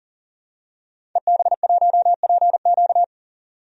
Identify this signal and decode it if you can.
Morse code, heard as a keyed tone, about 40 wpm, 700 Hz. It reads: EB1PQ
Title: DXCC-100 Call Signs - 5 Characters - 40wpm